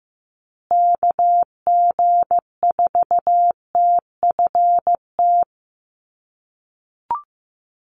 Morse code, heard as a keyed tone, about 15 wpm, 700 Hz. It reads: KG4TFT
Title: USA Call Signs Repeated 2x 15wpm